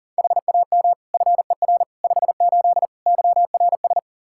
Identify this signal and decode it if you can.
Morse code, heard as a keyed tone, about 40 wpm, 700 Hz. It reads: HAM FER 58 YRS
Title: Practice QSO - Exchange 3 - More Info - 40wpm